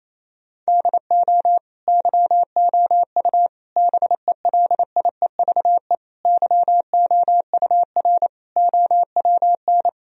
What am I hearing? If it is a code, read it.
Morse code, heard as a keyed tone, about 28 wpm, 700 Hz. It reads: DO YOU BELIEVE YOUR OWN
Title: Sentences from Top 1000 Words - Review Part 3 - 28wpm